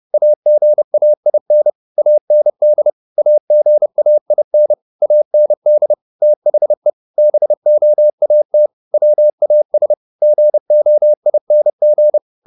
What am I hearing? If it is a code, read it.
Morse code, heard as a keyed tone, about 30 wpm, 600 Hz. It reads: AGAIN AND AGAIN AND THE BOAT WAS GOING